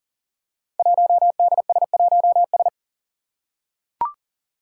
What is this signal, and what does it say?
Morse code, heard as a keyed tone, about 40 wpm, 700 Hz. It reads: 1DS1S
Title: Five Challenging Letter-Number Combinations 40wpm